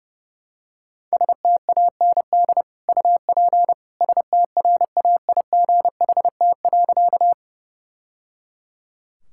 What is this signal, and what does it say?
Morse code, heard as a keyed tone, about 30 wpm, 700 Hz. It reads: STAND UP STRAIGHT.
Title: Sentences from Top 1000 Words - Review Part 6 - 30wpm